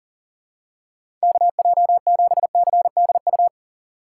Morse code, heard as a keyed tone, about 40 wpm, 700 Hz. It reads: KJ7CDU
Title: USA Call Signs - 3x Character Spacing - 40wpm